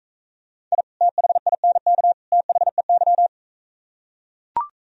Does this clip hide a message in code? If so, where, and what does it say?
Yes: Morse code from 0.7 to 3.3 s, heard as a keyed tone, about 42 wpm, 700 Hz. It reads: I THINK THEY